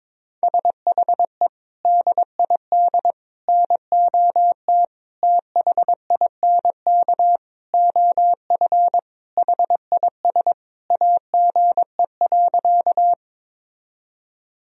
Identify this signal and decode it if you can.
Morse code, heard as a keyed tone, about 22 wpm, 700 Hz. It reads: SHE DID NOT THINK OF HIS AGE.
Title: Sentences from Top 1000 Words - Review Part 3 - 22wpm